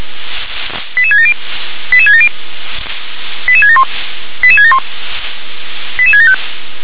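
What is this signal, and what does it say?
Five-tone selective call (ZVEI), heard as quick runs of five tones, tones of about 70 ms. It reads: 88670 88670 88671 88671 88675
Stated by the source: Five-tone ZVEI selective call 88776 received by SDR.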